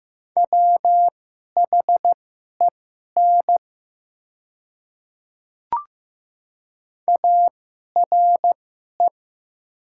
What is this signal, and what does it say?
Morse code, heard as a keyed tone, about 15 wpm, spaced as for 11 wpm, 700 Hz. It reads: WHEN ARE
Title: 2x Character Spacing - Top 100 Words - 15wpm